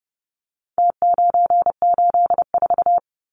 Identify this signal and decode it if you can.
Morse code, heard as a keyed tone, about 30 wpm, 700 Hz. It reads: T984
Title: Numbers - Contest Style - 4-Digit 30wpm